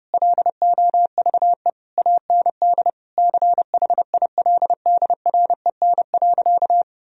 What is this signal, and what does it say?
Morse code, heard as a keyed tone, about 30 wpm, 700 Hz. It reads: LOVE AND CHILDREN.
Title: Sentences from Top 1000 Words - Review Part 3 - 30wpm